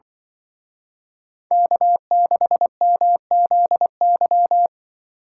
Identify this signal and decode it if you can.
Morse code, heard as a keyed tone, about 25 wpm, 700 Hz. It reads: K6MZY
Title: Speed Racing - USA 1x3 Call Signs 20wpm